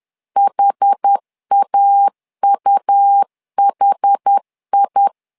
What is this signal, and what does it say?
Morse code, heard as a keyed tone, about 10 wpm, 800 Hz. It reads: HAUHI
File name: LEZIONI / 2 UHDN / 2 UHD 10wpm 1826